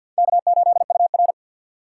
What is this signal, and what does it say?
Morse code, heard as a keyed tone, about 51 wpm, 700 Hz. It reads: K8UR
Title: USA Call Signs - 4x Character Spacing - 50wpm